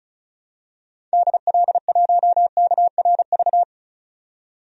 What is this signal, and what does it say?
Morse code, heard as a keyed tone, about 34 wpm, 700 Hz. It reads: DL1KRV
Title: DXCC-100 Call Signs - 6 Characters - 35wpm